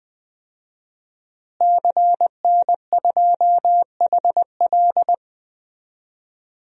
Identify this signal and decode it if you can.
Morse code, heard as a keyed tone, about 20 wpm, 700 Hz. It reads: CN2HL
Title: DXCC-100 Call Signs - 5 Characters - Repeated 3x - 20wpm